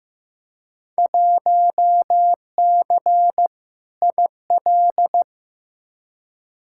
Morse code, heard as a keyed tone, about 15 wpm, 700 Hz. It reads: 1C IL
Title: Field Day Exchanges - 15wpm